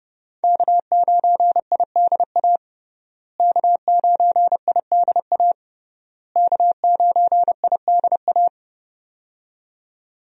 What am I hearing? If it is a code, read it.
Morse code, heard as a keyed tone, about 32 wpm, 700 Hz. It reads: K9IDA K9IDA K9IDA
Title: USA Call Signs - 5 Characters - Repeated 3x - 30wpm